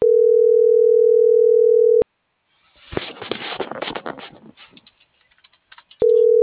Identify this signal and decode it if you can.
Ringback tone (North America), 440 + 480 Hz.